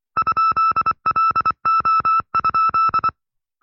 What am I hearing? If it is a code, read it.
Morse code, heard as a keyed tone, about 24 wpm, 1310 Hz. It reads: ?LO?
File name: FreeFiles / CW-practice / 20WPM-side2